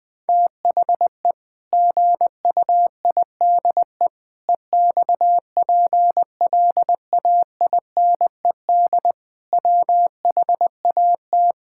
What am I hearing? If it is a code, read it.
Morse code, heard as a keyed tone, about 20 wpm, 700 Hz. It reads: THE GUIDE EXPLAINED WHAT
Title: Sentences with 8-Word Phrases - 20wpm